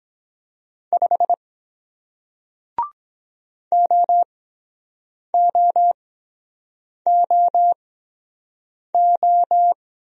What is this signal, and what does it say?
Morse code, heard as a keyed tone, about 22 wpm, spaced as for 7 wpm, 700 Hz. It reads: 5 OOOO